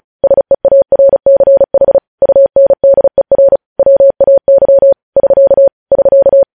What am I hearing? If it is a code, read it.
Morse code, heard as a keyed tone, about 36 wpm, 550 Hz. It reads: SEARCH UNDER WAY <SK> <SK>
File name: podcast / News headlines-1x-35x35